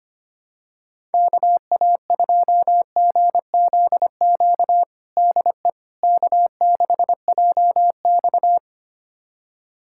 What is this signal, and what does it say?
Morse code, heard as a keyed tone, about 25 wpm, 700 Hz. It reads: KA2GZQ DE K6JX
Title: USA Callsign DE Callsign 25wpm